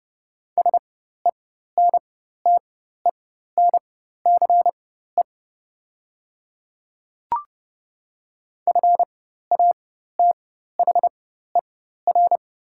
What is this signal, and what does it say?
Morse code, heard as a keyed tone, about 31 wpm, spaced as for 14 wpm, 690 Hz. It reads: SENTENCE FATHER
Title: 4x Character Spacing - Top 300 Words - 30wpm